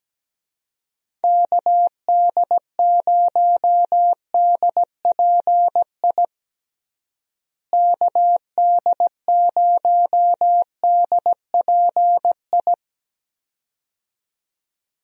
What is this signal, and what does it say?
Morse code, heard as a keyed tone, about 17 wpm, 700 Hz. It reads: KD0DPI KD0DPI
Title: USA Call Signs - 6 Characters - Repeated 2x - 17wpm